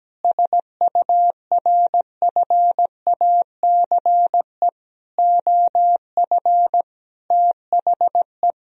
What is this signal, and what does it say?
Morse code, heard as a keyed tone, about 17 wpm, 700 Hz. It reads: SURFACE OF THE
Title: Sentences from Top 1000 Words - Review Part 3 - 17wpm